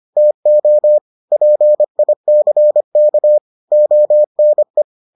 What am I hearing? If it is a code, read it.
Morse code, heard as a keyed tone, about 25 wpm, 600 Hz. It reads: TO PICK ONE